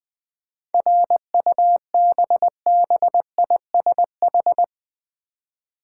Morse code, heard as a keyed tone, about 20 wpm, 700 Hz. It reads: RUBBISH
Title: 7 Letter Words - First 3 Letters Given - 20wpm